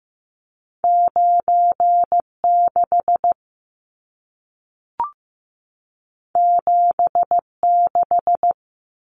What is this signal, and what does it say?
Morse code, heard as a keyed tone, about 15 wpm, 700 Hz. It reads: 96 76
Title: Numbers - 2-Digit 15wpm